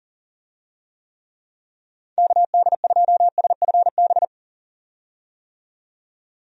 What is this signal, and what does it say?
Morse code, heard as a keyed tone, about 40 wpm, 700 Hz. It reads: KD2SFB
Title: USA Call Signs - 6 Characters - 40wpm